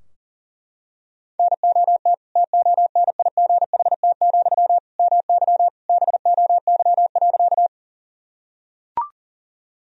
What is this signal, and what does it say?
Morse code, heard as a keyed tone, about 40 wpm, 700 Hz. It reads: NOT TONIGHT, MY BOY.
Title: Sentences from Top 1000 Words - Review Part 3 - 40wpm